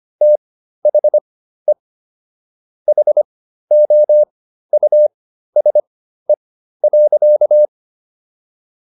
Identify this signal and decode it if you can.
Morse code, heard as a keyed tone, about 25 wpm, spaced as for 13 wpm, 600 Hz. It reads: THE HOUSE.